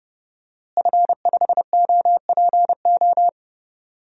Morse code, heard as a keyed tone, about 30 wpm, 700 Hz. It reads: F5OPO
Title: DXCC-100 Call Signs - 5 Characters - Repeated 3x - 30wpm